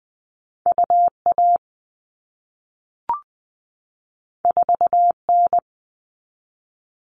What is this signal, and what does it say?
Morse code, heard as a keyed tone, about 20 wpm, 700 Hz. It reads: UA 4N
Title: All Cut Numbers - 2-Digit 20wpm